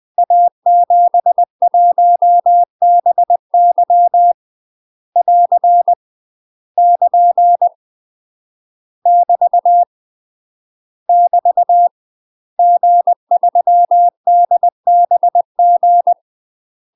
Morse code, heard as a keyed tone, about 20 wpm, 700 Hz. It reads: A71BY <AR> <KN> <BT> <BT> G3DBG
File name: UK MASTER QSOs TRAINING / 20WPM / UK MASTER QSOs 0034